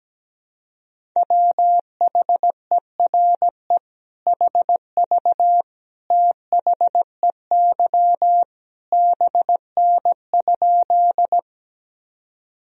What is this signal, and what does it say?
Morse code, heard as a keyed tone, about 17 wpm, 700 Hz. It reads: WHERE HV THEY BN?
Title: Sentences from Top 200 Words and Abbreviations 17wpm